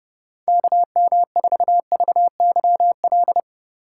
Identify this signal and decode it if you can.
Morse code, heard as a keyed tone, about 30 wpm, 700 Hz. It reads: KM4VYL
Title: USA Call Signs - 6 Characters - 8x Character Spacing - 30wpm